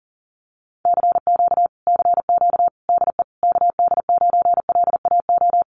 Morse code, heard as a keyed tone, about 40 wpm, 700 Hz. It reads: CQ CQ DE KD9LAO
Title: Common QSO Phrases Encore 40wpm